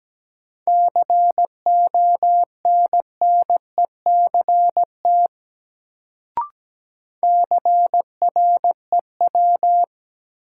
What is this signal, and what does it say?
Morse code, heard as a keyed tone, about 17 wpm, 700 Hz. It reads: CONNECT CREW
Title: Sentences from Top 1500 Words - New Word Drill - 17wpm